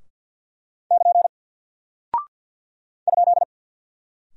Morse code, heard as a keyed tone, about 50 wpm, 700 Hz. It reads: <KN> ?
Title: Single Character (Letter:Number:Punctuation:Prosign) 50wpm